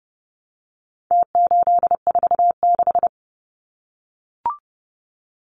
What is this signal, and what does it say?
Morse code, heard as a keyed tone, about 30 wpm, 700 Hz. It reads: T846
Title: Numbers - Contest Style - 4-Digit 30wpm